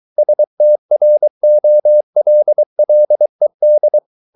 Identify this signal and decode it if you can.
Morse code, heard as a keyed tone, about 23 wpm, 600 Hz. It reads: STROLLED